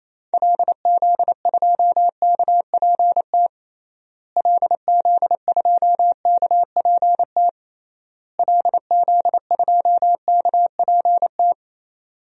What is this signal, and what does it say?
Morse code, heard as a keyed tone, about 28 wpm, 700 Hz. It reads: LZ2KPT LZ2KPT LZ2KPT
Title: DXCC-100 Call Signs - 6 Characters - Repeated 3x - 28wpm